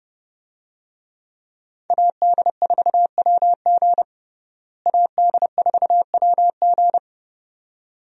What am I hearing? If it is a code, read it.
Morse code, heard as a keyed tone, about 30 wpm, 700 Hz. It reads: AD4WG AD4WG
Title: USA Call Signs - 5 Characters - Repeated 2x - 30wpm